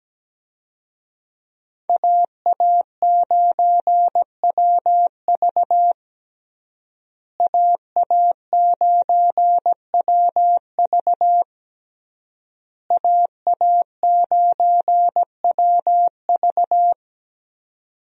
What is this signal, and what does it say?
Morse code, heard as a keyed tone, about 17 wpm, 700 Hz. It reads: AA9WV AA9WV AA9WV
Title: USA Call Signs - 5 Characters - Repeated 3x - 17wpm